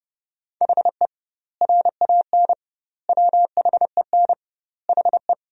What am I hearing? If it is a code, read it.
Morse code, heard as a keyed tone, about 30 wpm, 700 Hz. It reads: HE RAN WHEN HE
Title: 2x Word-Spacing - Sentences from Top 500 Words - 30wpm